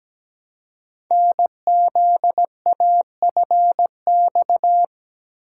Morse code, heard as a keyed tone, about 17 wpm, 700 Hz. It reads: NZAFX
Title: Code Groups - 5 Letters - 17wpm